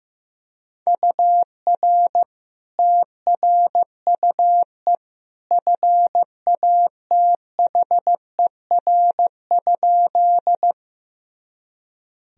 Morse code, heard as a keyed tone, about 16 wpm, 700 Hz. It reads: UR TRUE FATHER?
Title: Sentences from Top 400 Words and Abbreviations 15wpm